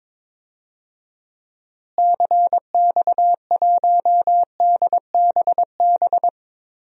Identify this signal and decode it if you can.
Morse code, heard as a keyed tone, about 22 wpm, 700 Hz. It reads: CX1DBB
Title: DXCC-100 Call Signs - 6 Characters - Repeated 2x - 22wpm